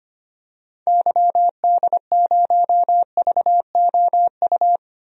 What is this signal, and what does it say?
Morse code, heard as a keyed tone, about 25 wpm, 700 Hz. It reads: YD0VOU
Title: DXCC-100 Call Signs - 6 Characters - Repeated 3x - 25wpm